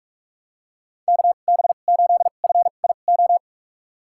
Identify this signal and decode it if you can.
Morse code, heard as a keyed tone, about 44 wpm, spaced as for 32 wpm, 700 Hz. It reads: KB8FIO
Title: USA Call Signs - 2x Character Spacing - 45wpm